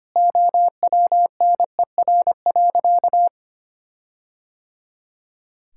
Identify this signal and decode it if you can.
Morse code, heard as a keyed tone, about 25 wpm, 700 Hz. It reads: OWNER.
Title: Sentences from Top 1600 Words 25wpm